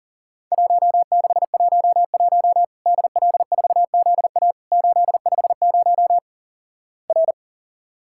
Morse code, heard as a keyed tone, about 40 wpm, 700 Hz. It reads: 1611 DL4ZA 850 R
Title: Worked All Europe DX Contest - QTC Exchange - 40wpm